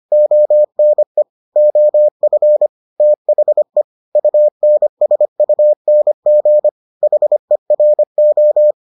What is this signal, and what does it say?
Morse code, heard as a keyed tone, about 25 wpm, 600 Hz. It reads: ONE OF THE UNSUNG HERO